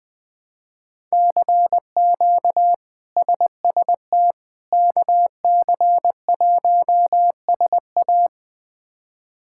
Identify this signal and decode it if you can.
Morse code, heard as a keyed tone, about 20 wpm, 700 Hz. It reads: CQ SST KC1SA
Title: K1USN SST QSO Practice - 20wpm